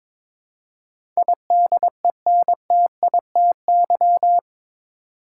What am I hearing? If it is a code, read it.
Morse code, heard as a keyed tone, about 22 wpm, 700 Hz. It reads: IDENTITY
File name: practice / 22wpm / Eight Letter Words - 22wpm